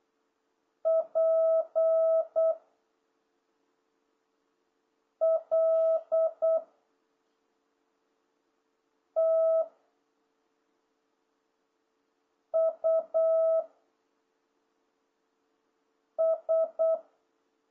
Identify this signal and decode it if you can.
Morse code, heard as a keyed tone, about 8 wpm, 660 Hz. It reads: P L T U S